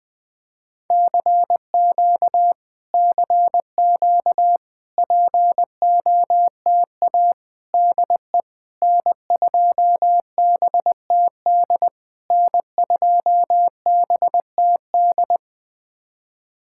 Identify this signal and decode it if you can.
Morse code, heard as a keyed tone, about 20 wpm, 700 Hz. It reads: CQ CQ POTA DE N2BTD N2BTD
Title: POTA QSO Practice - 20wpm